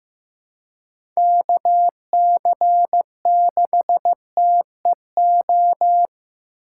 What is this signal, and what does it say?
Morse code, heard as a keyed tone, about 15 wpm, 700 Hz. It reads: KC6TEO
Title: USA Call Signs - 6 Characters - 3x Character Spacing - 15wpm